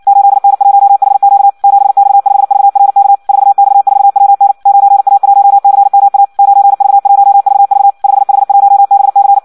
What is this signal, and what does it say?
Morse code, heard as a keyed tone, about 56 wpm, 800 Hz. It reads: 8N1HQ 7X5VRK 4X4WN 9A1CMA 9V1VV 5H1BC
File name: Audio practica CW / qrq-tx-58 2